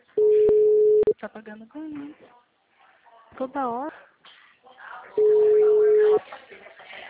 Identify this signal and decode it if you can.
Ringback tone (Europe, 425 Hz), with speech.